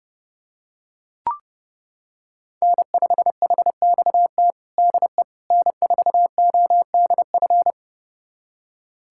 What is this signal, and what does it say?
Morse code, heard as a keyed tone, about 30 wpm, 700 Hz. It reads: N5HXT DE N4ODF
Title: USA Callsign DE Callsign - 5 Characters - 30wpm